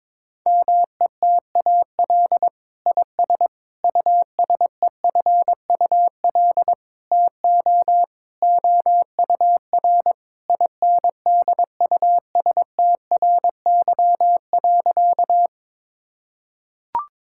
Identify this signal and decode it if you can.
Morse code, heard as a keyed tone, about 21 wpm, 700 Hz. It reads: METAL IS USEFUL TO OUR INDUSTRY.
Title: Sentences from Top 1000 Words - Review Part 6 - 22wpm